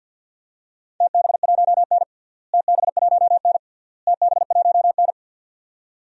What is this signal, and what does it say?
Morse code, heard as a keyed tone, about 45 wpm, 700 Hz. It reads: TB1N TB1N TB1N
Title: DXCC-100 Call Signs - 4 Characters - Repeated 3x - 50wpm